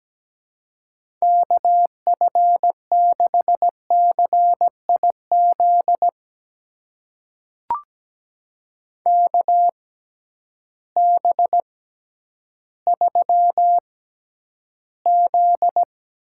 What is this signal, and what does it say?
Morse code, heard as a keyed tone, about 17 wpm, spaced as for 10 wpm, 700 Hz. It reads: KF6CIZ KB3Z
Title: USA Call Signs - 6 Characters - 6x Character Spacing - 17wpm